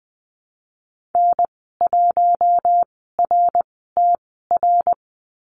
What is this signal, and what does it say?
Morse code, heard as a keyed tone, about 20 wpm, spaced as for 14 wpm, 700 Hz. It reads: N1RTR
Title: 076 Lesson 13g - Letter L - Call Signs 20wpm